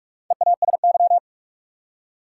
Morse code, heard as a keyed tone, about 46 wpm, 700 Hz. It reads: EASY
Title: Four Letter Words - 45wpm